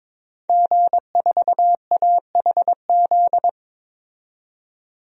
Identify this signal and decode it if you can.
Morse code, heard as a keyed tone, about 22 wpm, 700 Hz. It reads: G4AHZ
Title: DXCC-100 Call Signs - 5 Characters - 22wpm